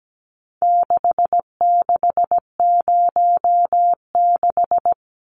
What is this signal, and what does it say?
Morse code, heard as a keyed tone, about 17 wpm, 700 Hz. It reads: 6606
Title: Numbers - 4-Digit 17wpm